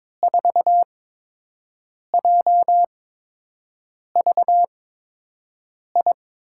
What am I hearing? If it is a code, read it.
Morse code, heard as a keyed tone, about 21 wpm, spaced as for 6 wpm, 700 Hz. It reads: 4JVI
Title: USA Call Signs - 6 Characters - 8x Character Spacing - 22wpm